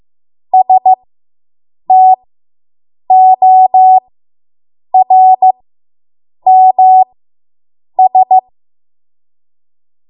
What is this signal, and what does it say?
Morse code, heard as a keyed tone, about 15 wpm, spaced as for 7 wpm, 750 Hz. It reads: STORMS